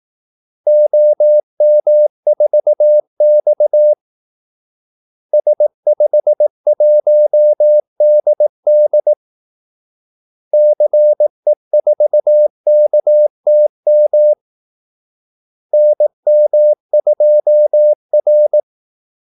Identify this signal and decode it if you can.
Morse code, heard as a keyed tone, about 18 wpm, 600 Hz. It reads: OM4X S51DD CE4KTM NM2R
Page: Kaldesignaler 18 wpm